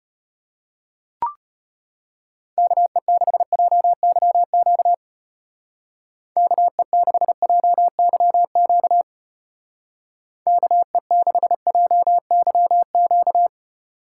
Morse code, heard as a keyed tone, about 31 wpm, 700 Hz. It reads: KE6JYQ KE6JYQ KE6JYQ
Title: Speed Racing - USA Call Signs 25wpm